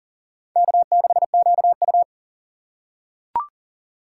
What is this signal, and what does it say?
Morse code, heard as a keyed tone, about 40 wpm, 700 Hz. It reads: K6QU
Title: USA Call Signs - 4 Characters - 4x Character Spacing - 40wpm